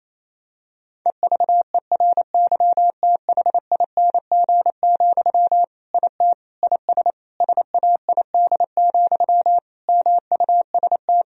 Morse code, heard as a keyed tone, about 27 wpm, 700 Hz. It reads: EVERYTHING, IT IS SAID, MUST
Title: Sentences from Top 1000 Words - Review Part 1 - 28wpm